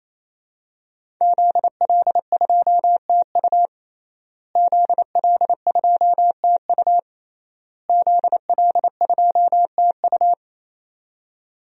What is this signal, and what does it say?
Morse code, heard as a keyed tone, about 28 wpm, 700 Hz. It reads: ZL2TU ZL2TU ZL2TU
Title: DXCC-100 Call Signs - 5 Characters - Repeated 3x - 28wpm